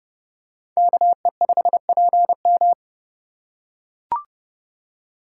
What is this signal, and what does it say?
Morse code, heard as a keyed tone, about 29 wpm, 700 Hz. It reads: KE5PM